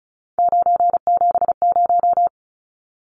Morse code, heard as a keyed tone, about 34 wpm, 700 Hz. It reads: 970
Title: Numbers - 3-Digit 35wpm